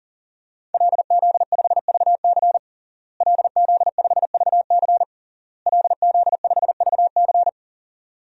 Morse code, heard as a keyed tone, about 40 wpm, 700 Hz. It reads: LZ5VC LZ5VC LZ5VC
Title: DXCC-100 Call Signs - 5 Characters - Repeated 3x - 40wpm